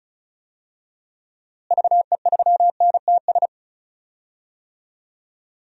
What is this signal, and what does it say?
Morse code, heard as a keyed tone, about 34 wpm, 700 Hz. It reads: VE3NTS
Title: DXCC-100 Call Signs - 6 Characters - Repeated 3x - 35wpm